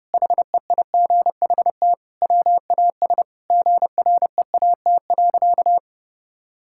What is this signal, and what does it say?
Morse code, heard as a keyed tone, about 30 wpm, 700 Hz. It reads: HEIGHT WAS GREAT.
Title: Sentences from Top 1600 Words 30wpm